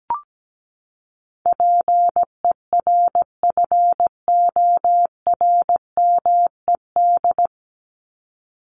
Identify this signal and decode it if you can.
Morse code, heard as a keyed tone, about 17 wpm, 700 Hz. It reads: PERFORMED
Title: ED Suffix - Top 100 Words with Base Words 17wpm